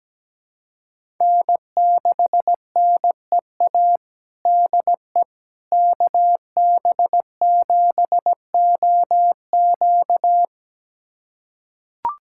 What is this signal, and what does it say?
Morse code, heard as a keyed tone, about 17 wpm, 700 Hz. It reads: N6NEA DE KB7OQ
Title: USA Callsign DE Callsign - 5 Characters - 17wpm